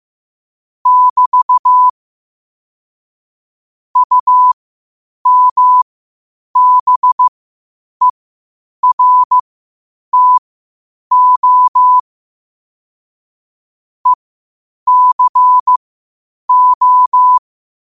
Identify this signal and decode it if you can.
Morse code, heard as a keyed tone, about 15 wpm, spaced as for 8 wpm, 1000 Hz. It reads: <BT> UMBERTO ECO
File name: qotdfiles / 8wpm